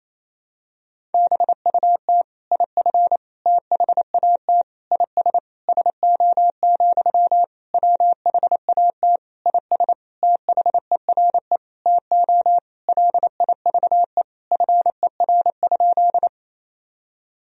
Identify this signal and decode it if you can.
Morse code, heard as a keyed tone, about 28 wpm, 700 Hz. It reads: BUT IF THAT IS SO, WHAT IS THERE TO LIVE FER?
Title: Sentences from Top 200 Words and Abbreviations 28wpm